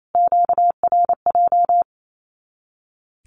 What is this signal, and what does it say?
Morse code, heard as a keyed tone, about 28 wpm, 700 Hz. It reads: QRJ
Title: Q-Codes Repeated 3x 28wpm